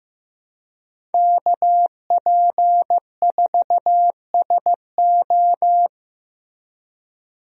Morse code, heard as a keyed tone, about 15 wpm, 700 Hz. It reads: KP4SO
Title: DXCC-100 Call Signs - 5 Characters - Repeated 2x - 15wpm